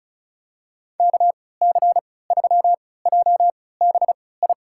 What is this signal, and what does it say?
Morse code, heard as a keyed tone, about 36 wpm, spaced as for 20 wpm, 700 Hz. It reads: KC3JBI
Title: USA Call Signs - 6 Characters - 3x Character Spacing - 35wpm